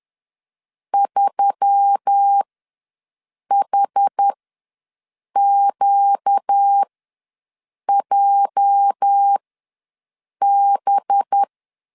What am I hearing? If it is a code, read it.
Morse code, heard as a keyed tone, about 10 wpm, spaced as for 6 wpm, 800 Hz. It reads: 3HQJB
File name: LEZIONI / 9 2 ws 10wpm 4130